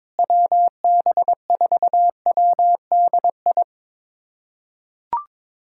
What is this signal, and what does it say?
Morse code, heard as a keyed tone, about 22 wpm, 700 Hz. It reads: WB4WDI
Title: USA Callsign DE Callsign - 6 Characters - 22wpm